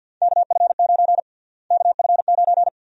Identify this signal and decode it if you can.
Morse code, heard as a keyed tone, about 52 wpm, 700 Hz. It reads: KF9 KF9
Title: USA Call Sign Prefixes - Repeated 2x - 50wpm